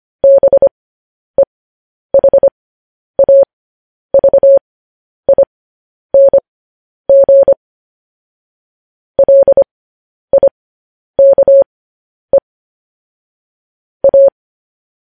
Morse code, heard as a keyed tone, about 25 wpm, spaced as for 10 wpm, 550 Hz. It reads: BEHAVING LIKE A
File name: podcast / News headlines-1x-25x10